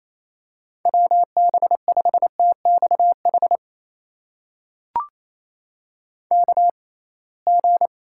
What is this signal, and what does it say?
Morse code, heard as a keyed tone, about 27 wpm, spaced as for 21 wpm, 700 Hz. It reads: WB5TXH KG
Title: USA Call Signs - 6 Characters - 6x Character Spacing - 28wpm